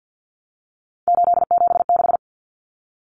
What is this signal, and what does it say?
Morse code, heard as a keyed tone, about 50 wpm, 700 Hz. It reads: 876
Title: Numbers - 3-Digit 50wpm